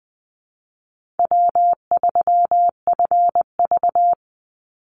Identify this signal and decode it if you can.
Morse code, heard as a keyed tone, about 20 wpm, 700 Hz. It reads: W3FV
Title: Easy Call Signs - 20wpm